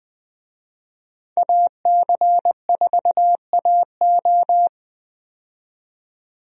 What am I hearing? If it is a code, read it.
Morse code, heard as a keyed tone, about 20 wpm, 700 Hz. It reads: AC4AO
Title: USA Call Signs - 5 Characters - 20wpm